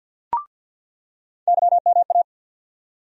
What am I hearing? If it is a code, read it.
Morse code, heard as a keyed tone, about 46 wpm, 700 Hz. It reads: YMA
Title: USA Call Sign Postfixes - 3 Letters 50wpm